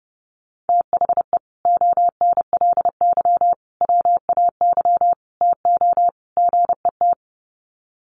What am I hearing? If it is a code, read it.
Morse code, heard as a keyed tone, about 30 wpm, 700 Hz. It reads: THE ONLY WAY TO GET
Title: Sets of 5 Words Encore2 30wpm